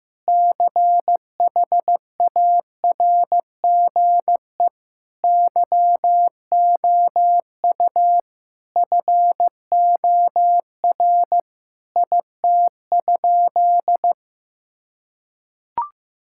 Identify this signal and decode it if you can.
Morse code, heard as a keyed tone, about 16 wpm, 700 Hz. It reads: CHARGE YOU FOR IT?
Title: Sentences from Top 1000 Words - Review Part 6 - 15wpm